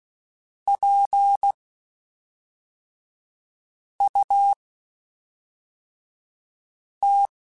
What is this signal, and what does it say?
Morse code, heard as a keyed tone, about 16 wpm, spaced as for 3 wpm, 800 Hz. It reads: PUT
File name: Spaced / Chuckles Spaced-mp3s / BrainMouth sp @5